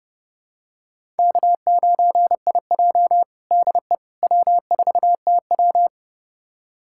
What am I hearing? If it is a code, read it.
Morse code, heard as a keyed tone, about 30 wpm, 700 Hz. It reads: K9IJ DE W4TW
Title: USA Callsign DE Callsign - 4 Characters - 30wpm